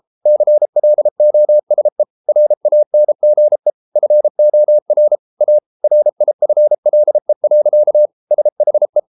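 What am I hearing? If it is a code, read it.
Morse code, heard as a keyed tone, about 33 wpm, 600 Hz. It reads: CLOSE RANGE FOR A RIFLE. SHE
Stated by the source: Morse code practice at 32 wpm from CW med Gnister, reading Ken Follett's novel Never in English.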